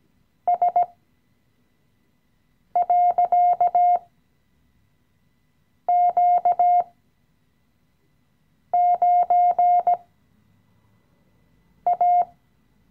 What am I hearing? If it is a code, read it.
Morse code, heard as a keyed tone, about 17 wpm, spaced as for 4 wpm, 710 Hz. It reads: S.Q9A
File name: CW-Online-Kurs / Anfänger / Lektion 016